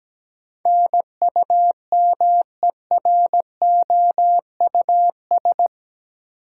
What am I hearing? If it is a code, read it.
Morse code, heard as a keyed tone, about 17 wpm, 700 Hz. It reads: NUMEROUS
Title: Sentences from Top 1400 Words - New Word Drill - 17wpm